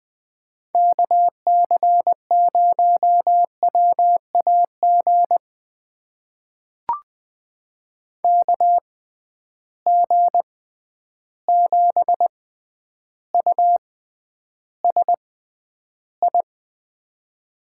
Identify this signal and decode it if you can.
Morse code, heard as a keyed tone, about 20 wpm, spaced as for 10 wpm, 700 Hz. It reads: KC0WAG KG7USI
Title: USA Call Signs - 6 Characters - 6x Character Spacing - 20wpm